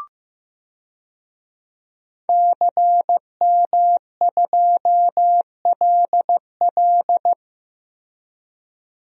Morse code, heard as a keyed tone, about 15 wpm, 700 Hz. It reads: CM2LL
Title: DXCC-100 Call Signs - 5 Characters - Repeated 2x - 15wpm